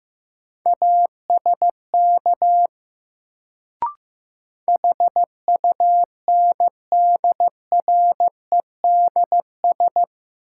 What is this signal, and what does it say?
Morse code, heard as a keyed tone, about 15 wpm, 700 Hz. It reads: ASK HUNDREDS
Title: Sets of 4 Words - New Word Drill - 15wpm